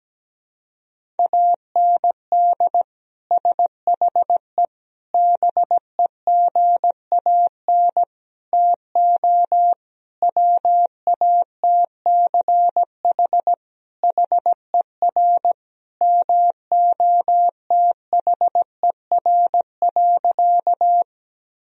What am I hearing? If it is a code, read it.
Morse code, heard as a keyed tone, about 17 wpm, 700 Hz. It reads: AND SHE BEGAN TO WATCH HER MOTHER.
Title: Sentences from Top 1000 Words - Review Part 1 - 17wpm